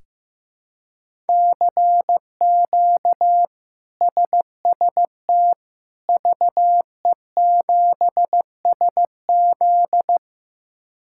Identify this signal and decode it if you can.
Morse code, heard as a keyed tone, about 15 wpm, 700 Hz. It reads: CQ SST VE7SZ
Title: K1USN SST QSO Practice - 15wpm